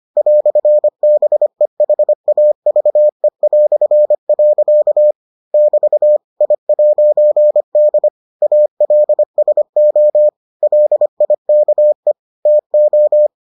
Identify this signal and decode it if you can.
Morse code, heard as a keyed tone, about 25 wpm, 600 Hz. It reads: "BEHAVE". <BT> I'D ALSO LIKE TO